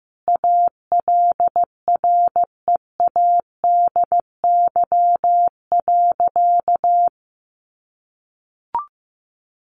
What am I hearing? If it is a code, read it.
Morse code, heard as a keyed tone, about 14 wpm, 700 Hz. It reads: ALREADY.
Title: Sentences from Top 700 Words 15wpm